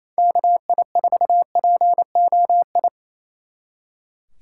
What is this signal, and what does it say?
Morse code, heard as a keyed tone, about 36 wpm, 700 Hz. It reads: KI4POI